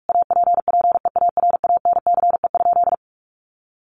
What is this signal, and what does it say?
Morse code, heard as a keyed tone, about 44 wpm, 700 Hz. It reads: APPEARANCE?
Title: Sentences from Top 1100 Words 45wpm